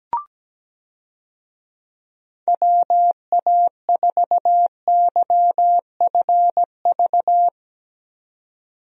Morse code, heard as a keyed tone, about 17 wpm, 700 Hz. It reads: WA4YFV
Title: USA Call Signs - 6 Characters - 17wpm